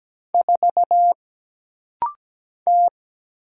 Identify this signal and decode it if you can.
Morse code, heard as a keyed tone, about 17 wpm, 700 Hz. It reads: 4 T
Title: Single Letter-Number - Rapid-Fire 17wpm